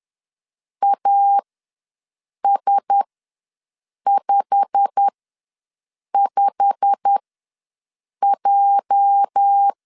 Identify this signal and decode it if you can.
Morse code, heard as a keyed tone, about 10 wpm, spaced as for 6 wpm, 800 Hz. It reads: AS55J
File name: LEZIONI / 8 YJ45 / 8 YJ45 ws 10wpm 3843